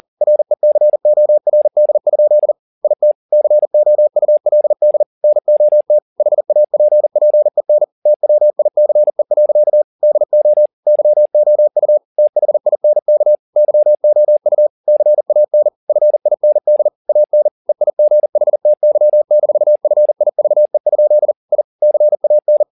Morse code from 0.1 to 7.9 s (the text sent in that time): RECORD? IT COULD NOT HAPPEN